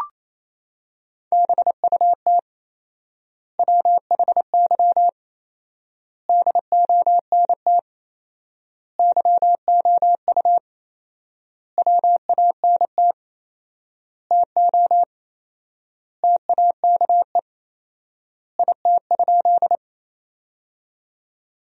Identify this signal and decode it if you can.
Morse code, heard as a keyed tone, about 27 wpm, 700 Hz. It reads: BUT WHY DONT YOU WANT TO TAKE IT?
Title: 4x Word-Spacing - Sentences from Top 300 Words - 28wpm